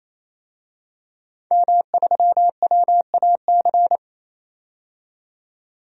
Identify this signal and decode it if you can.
Morse code, heard as a keyed tone, about 28 wpm, 700 Hz. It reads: M3WAC
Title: DXCC-100 Call Signs - 5 Characters - 28wpm